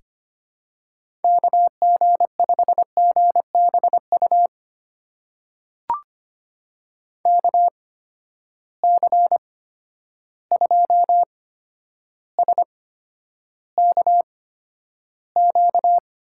Morse code, heard as a keyed tone, about 25 wpm, spaced as for 11 wpm, 700 Hz. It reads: KG5GBU KC2SKQ
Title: USA Call Signs - 6 Characters - 8x Character Spacing - 25wpm